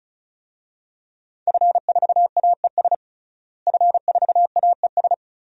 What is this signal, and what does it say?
Morse code, heard as a keyed tone, about 35 wpm, 700 Hz. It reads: F4AES F4AES
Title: DXCC-100 Call Signs - 5 Characters - Repeated 2x - 35wpm